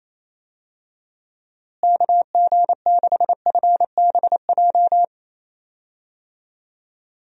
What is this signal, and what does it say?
Morse code, heard as a keyed tone, about 28 wpm, 700 Hz. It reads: KG6FBJ
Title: USA Call Signs - 6 Characters - Repeated 2x - 28wpm